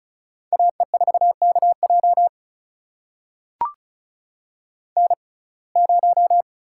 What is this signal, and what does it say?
Morse code, heard as a keyed tone, about 36 wpm, spaced as for 26 wpm, 700 Hz. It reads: AE4KJ N0
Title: USA Call Signs - 5 Characters - 6x Character Spacing - 35wpm